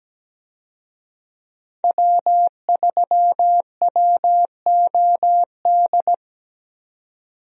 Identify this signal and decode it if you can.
Morse code, heard as a keyed tone, about 17 wpm, 700 Hz. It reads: W3WOD
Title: USA Call Signs - 5 Characters - 17wpm